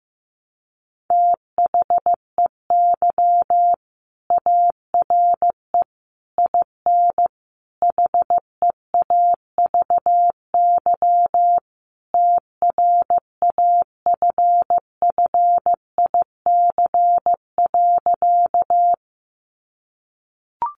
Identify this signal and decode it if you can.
Morse code, heard as a keyed tone, about 15 wpm, 700 Hz. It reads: THEY ARE IN HEAVY TRAFFIC.
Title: Sentences from Top 1300 Words 15wpm